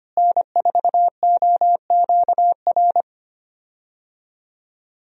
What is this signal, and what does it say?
Morse code, heard as a keyed tone, about 25 wpm, 700 Hz. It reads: N4OQR
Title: USA Call Signs - 5 Characters - Repeated 2x - 25wpm